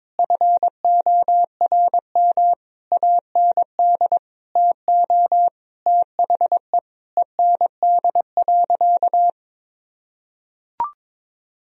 Morse code, heard as a keyed tone, about 22 wpm, 700 Hz. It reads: FORM AND TO THE END.
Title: Sentences from Top 1000 Words - Review Part 1 - 22wpm